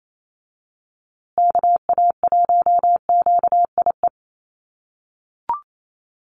Morse code, heard as a keyed tone, about 28 wpm, 700 Hz. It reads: KA1QIE
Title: USA Call Signs Repeated 3x 28wpm